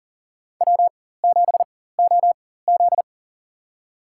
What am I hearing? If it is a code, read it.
Morse code, heard as a keyed tone, about 40 wpm, spaced as for 19 wpm, 700 Hz. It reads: W7OZ
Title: USA Call Signs - 4 Characters - 4x Character Spacing - 40wpm